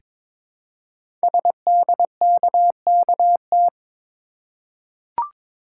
Morse code, heard as a keyed tone, about 22 wpm, 700 Hz. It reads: SDKKT
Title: Code Groups - 5 Letters - 22wpm